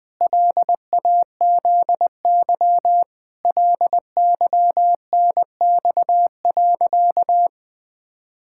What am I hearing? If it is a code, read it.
Morse code, heard as a keyed tone, about 20 wpm, 700 Hz. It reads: LAZY LYNX.
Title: Pangram Challenge v2 - 20wpm